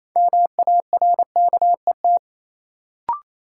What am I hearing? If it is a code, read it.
Morse code, heard as a keyed tone, about 28 wpm, 700 Hz. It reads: MARKET
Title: Sentences from Top 600 Words - New Word Drill - 28wpm